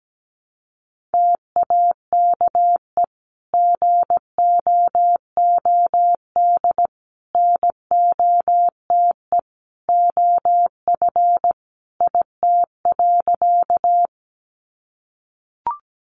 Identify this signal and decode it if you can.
Morse code, heard as a keyed tone, about 17 wpm, 700 Hz. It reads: TAKE GOOD NOTE OF IT.
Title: Sentences from Top 500 Words 17wpm